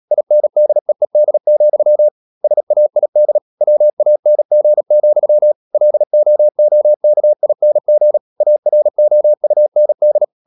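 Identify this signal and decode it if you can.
Morse code, heard as a keyed tone, about 37 wpm, 600 Hz. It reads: INDEED, SAID WANG, LOOKING AROUND
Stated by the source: Morse code practice at 35–39 wpm sending English text of Ken Follett's novel Never, from CW med Gnister.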